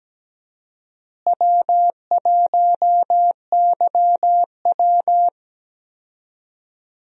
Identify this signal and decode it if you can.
Morse code, heard as a keyed tone, about 17 wpm, 700 Hz. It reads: W1YW
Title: USA Call Signs - 4 Characters - Repeated 2x - 17wpm